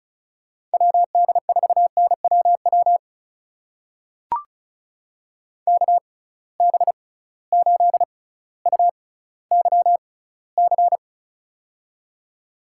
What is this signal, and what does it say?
Morse code, heard as a keyed tone, about 36 wpm, spaced as for 18 wpm, 700 Hz. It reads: WD4NWW KB8UYC
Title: USA Call Signs - 6 Characters - 6x Character Spacing - 35wpm